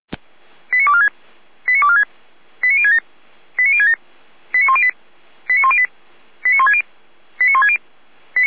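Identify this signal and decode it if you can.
Five-tone selective call (ZVEI), heard as quick runs of five tones, tones of about 70 ms. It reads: 89246 89246 89067 89067 89108 89108 89160 89160
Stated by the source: ZVEI five-tone call 89078 received by SDR, archive path 455.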